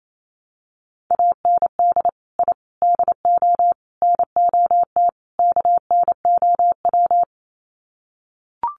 Morse code, heard as a keyed tone, about 28 wpm, 700 Hz. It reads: AND I DO NOT KNOW
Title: Sets of 5 Words 28wpm